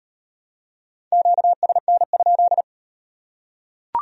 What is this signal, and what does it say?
Morse code, heard as a keyed tone, about 36 wpm, 700 Hz. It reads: QSN?